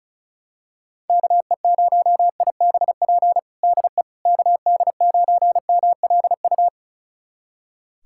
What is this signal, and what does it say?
Morse code, heard as a keyed tone, about 34 wpm, 700 Hz. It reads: KE0IBP DE KD9MLU
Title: USA Callsign DE Callsign - 6 Characters - 35wpm